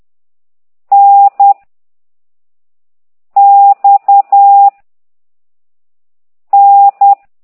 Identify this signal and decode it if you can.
Morse code, heard as a keyed tone, about 10 wpm, 800 Hz. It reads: N X N